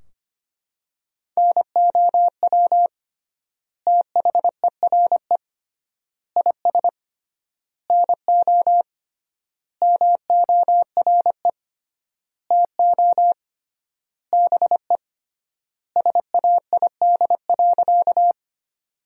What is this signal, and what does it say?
Morse code, heard as a keyed tone, about 25 wpm, 700 Hz. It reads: NOW THERE IS NO MORE TO BE SAID.
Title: 3x Word-Spacing - Sentences from Top 100 Words - 25wpm